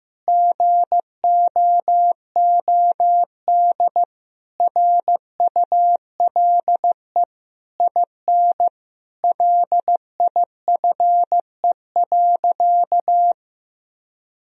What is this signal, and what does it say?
Morse code, heard as a keyed tone, about 15 wpm, 700 Hz. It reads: GOOD RULE IN LIFE.
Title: Sentences from Top 1000 Words - Review Part 2 - 15wpm